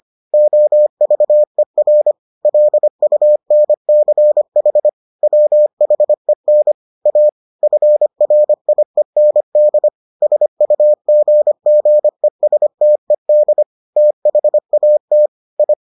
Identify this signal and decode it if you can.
Morse code, heard as a keyed tone, about 25 wpm, 600 Hz. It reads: OVER LUNCH WHEN A FRIEND SUGGESTED THAT I